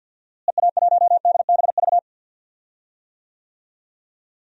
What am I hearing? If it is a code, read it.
Morse code, heard as a keyed tone, about 48 wpm, 700 Hz. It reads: EA1DBV
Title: DXCC-100 Call Signs - 6 Characters - Repeated 3x - 50wpm